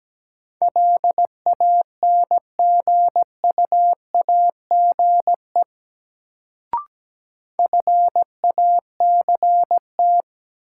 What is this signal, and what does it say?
Morse code, heard as a keyed tone, about 17 wpm, 700 Hz. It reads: LANGUAGE FACT
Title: Sentences from Top 500 Words - New Word Drill - 17wpm